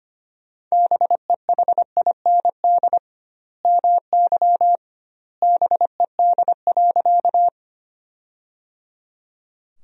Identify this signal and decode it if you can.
Morse code, heard as a keyed tone, about 25 wpm, 700 Hz. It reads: BEHIND MY BED.
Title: 2x Word-Spacing - Sentences from Top 500 Words - 25wpm